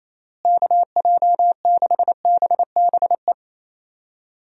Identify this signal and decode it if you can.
Morse code, heard as a keyed tone, about 28 wpm, 700 Hz. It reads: KJ6BBE
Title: USA Call Signs - 6 Characters - Repeated 3x - 28wpm